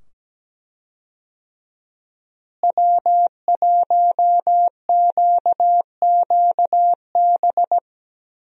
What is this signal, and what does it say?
Morse code, heard as a keyed tone, about 17 wpm, 700 Hz. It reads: W1QQB
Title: USA Call Signs - 5 Characters - Repeated 2x - 17wpm